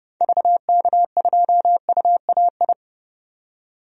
Morse code, heard as a keyed tone, about 30 wpm, 700 Hz. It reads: VK2UAI
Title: DXCC-100 Call Signs - 6 Characters - 30wpm